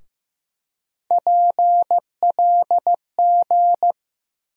Morse code, heard as a keyed tone, about 15 wpm, 700 Hz. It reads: PLG
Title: USA Call Sign Postfixes - 3 Letters 15wpm